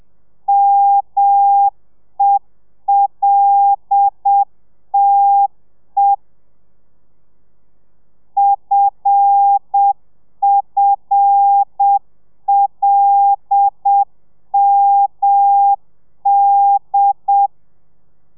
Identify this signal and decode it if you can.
Morse code, heard as a keyed tone, about 7 wpm, 800 Hz. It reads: MELTE FFLMD
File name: telegrafia / i2tfj / Lez5 30